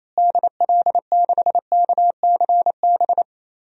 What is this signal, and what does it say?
Morse code, heard as a keyed tone, about 28 wpm, 700 Hz. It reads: DL6KCB
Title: DXCC-100 Call Signs - 6 Characters - Repeated 3x - 28wpm